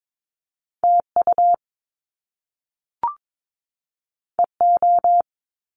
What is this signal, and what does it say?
Morse code, heard as a keyed tone, about 22 wpm, 700 Hz. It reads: TU EO
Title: Top Two Letter Combinations in Words 22wpm